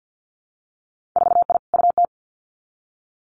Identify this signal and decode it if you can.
Morse code, heard as a keyed tone, about 50 wpm, 700 Hz. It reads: 4I UT
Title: Winter Field Day Exchanges Encore 50wpm